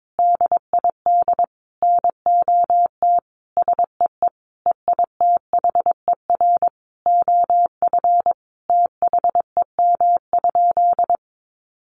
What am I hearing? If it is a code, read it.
Morse code, heard as a keyed tone, about 22 wpm, 700 Hz. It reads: DID NOT SEE EITHER OF THEM?
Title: Sentences from Top 600 Words 22wpm